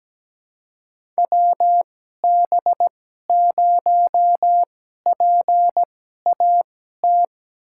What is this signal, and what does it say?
Morse code, heard as a keyed tone, about 17 wpm, spaced as for 12 wpm, 700 Hz. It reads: WB0PAT
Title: USA Call Signs - 6 Characters - 2x Character Spacing - 17wpm